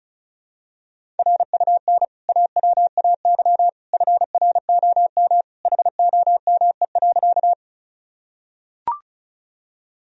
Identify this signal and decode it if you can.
Morse code, heard as a keyed tone, about 36 wpm, 700 Hz. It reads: RUN AWAY FROM HOME.
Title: Sentences from Top 1000 Words - Review Part 1 - 35wpm